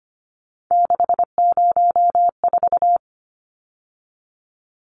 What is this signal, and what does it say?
Morse code, heard as a keyed tone, about 25 wpm, 700 Hz. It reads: 604
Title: Numbers - 3-Digit 25wpm